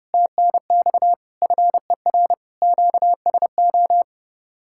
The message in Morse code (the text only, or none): TNX FER QSO